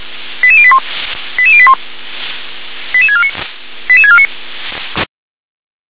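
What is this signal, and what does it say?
Five-tone selective call (ZVEI), heard as quick runs of five tones, tones of about 70 ms. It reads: 88081 88081 88649 88649
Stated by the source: ZVEI five-tone selective call 88126 received by SDR.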